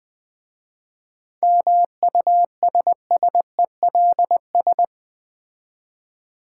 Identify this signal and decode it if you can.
Morse code, heard as a keyed tone, about 21 wpm, 700 Hz. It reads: MUSSELS
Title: Difficult Words with Hints - 20wpm